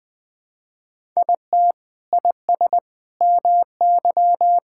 Morse code, heard as a keyed tone, about 21 wpm, 700 Hz. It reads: IT IS MY
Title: Sentences from Top 1000 Words - Review Part 5 - 20wpm